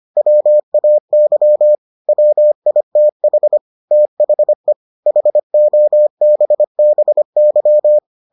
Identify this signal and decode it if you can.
Morse code, heard as a keyed tone, about 25 wpm, 600 Hz. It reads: WAY WITH THE HOBBY